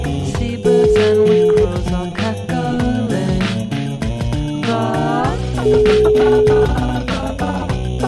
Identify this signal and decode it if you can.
Ringback tone (Europe, 425 Hz), with noise.